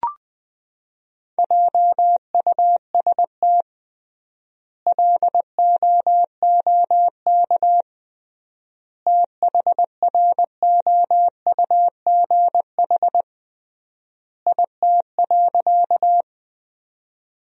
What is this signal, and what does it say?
Morse code, heard as a keyed tone, about 20 wpm, 700 Hz. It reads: JUST LOOK THROUGH IT.
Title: 3x Word-Spacing - Sentences from Top 200 Words - 20wpm